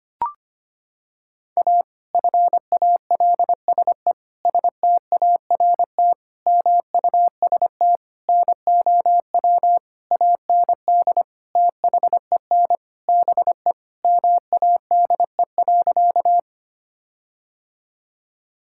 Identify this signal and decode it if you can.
Morse code, heard as a keyed tone, about 25 wpm, 700 Hz. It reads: A FALSE START MUST NOW AND THEN BE MADE.
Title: Sentences from Top 1600 Words 25wpm